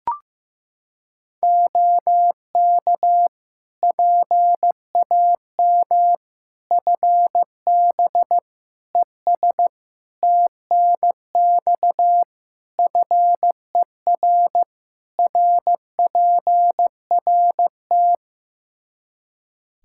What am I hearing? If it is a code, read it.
Morse code, heard as a keyed tone, about 15 wpm, 700 Hz. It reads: OK PAM FB ES TNX FER RPRT
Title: K1USN SST QSO Practice 15wpm